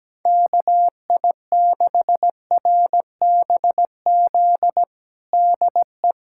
Morse code, heard as a keyed tone, about 17 wpm, 700 Hz. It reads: KI6RBZ DE
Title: USA Callsign DE Callsign - 6 Characters - 17wpm